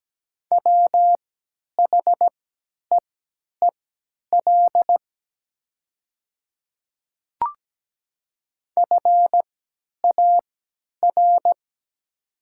Morse code, heard as a keyed tone, about 17 wpm, spaced as for 10 wpm, 700 Hz. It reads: WHEEL FAR
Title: 3x Character Spacing - Top 500 Words - 17wpm